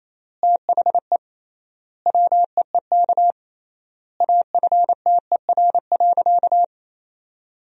Morse code, heard as a keyed tone, about 28 wpm, 700 Hz. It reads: THE WEEK AFTER.
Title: 3x Word-Spacing - Sentences from Top 500 Words - 28wpm